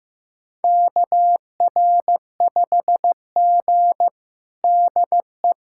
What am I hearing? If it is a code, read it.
Morse code, heard as a keyed tone, about 15 wpm, 700 Hz. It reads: KR5G DE
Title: USA Callsign DE Callsign - 4 Characters - 15wpm